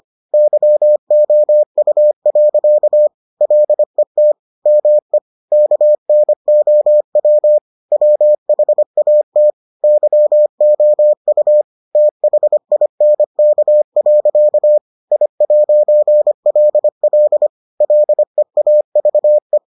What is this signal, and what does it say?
Morse code, heard as a keyed tone, about 25 wpm, 600 Hz. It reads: YOU. LET ME KNOW WHAT YOU THINK. I'LL LEAVE